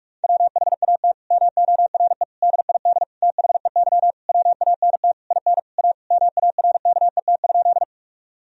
Morse code, heard as a keyed tone, about 46 wpm, 700 Hz. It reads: WHAT MORE DID THEY WANT IN A MARKET?
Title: Sentences from Top 1000 Words - Review Part 3 - 45wpm